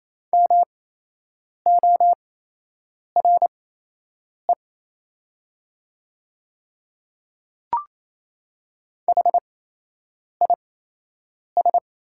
Morse code, heard as a keyed tone, about 28 wpm, spaced as for 8 wpm, 690 Hz. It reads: MORE HIS
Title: 8x Character Spacing - Top 100 Words - 28wpm